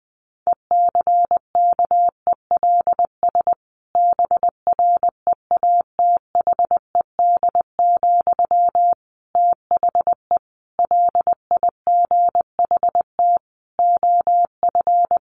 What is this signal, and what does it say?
Morse code, heard as a keyed tone, about 20 wpm, 700 Hz. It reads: ECKELS BREATHED, THE LIGHT OF
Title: A Sound Of Thunder 20wpm